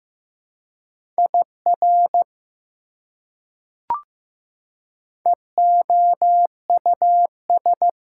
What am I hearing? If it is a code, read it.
Morse code, heard as a keyed tone, about 15 wpm, 700 Hz. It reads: IR EOUS
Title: Prefixes and Suffixes 15wpm